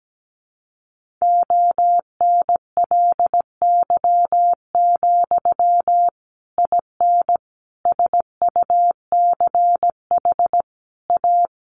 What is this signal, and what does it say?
Morse code, heard as a keyed tone, about 17 wpm, 700 Hz. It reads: ONLY, IN SUCH A
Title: Sentences from Top 1500 Words 17wpm